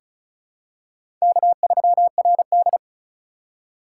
Morse code, heard as a keyed tone, about 36 wpm, 700 Hz. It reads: K3RD
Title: USA Call Signs - 4x Character Spacing - 35wpm